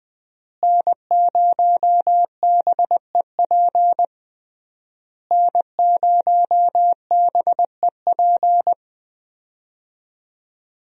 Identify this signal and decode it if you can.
Morse code, heard as a keyed tone, about 20 wpm, 700 Hz. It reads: N0BEP N0BEP
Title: USA Call Signs - 5 Characters - Repeated 3x - 20wpm